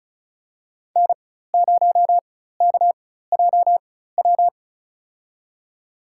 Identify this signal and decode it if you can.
Morse code, heard as a keyed tone, about 34 wpm, spaced as for 16 wpm, 700 Hz. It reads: N0KJW
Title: USA Call Signs - 4x Character Spacing - 35wpm